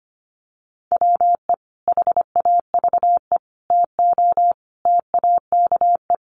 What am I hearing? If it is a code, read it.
Morse code, heard as a keyed tone, about 25 wpm, 700 Hz. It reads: WE HAVE TO TAKE
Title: Sets of 5 Words Encore 25wpm